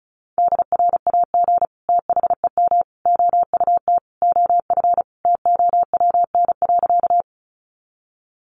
Morse code, heard as a keyed tone, about 34 wpm, 700 Hz. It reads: DRAG THEM OUT OF TOWN.
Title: Sentences from Top 1500 Words 35wpm